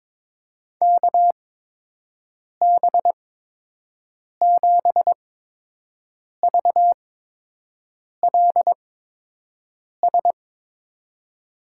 Morse code, heard as a keyed tone, about 22 wpm, spaced as for 6 wpm, 700 Hz. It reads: KB7VLS
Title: USA Call Signs - 6 Characters - 8x Character Spacing - 22wpm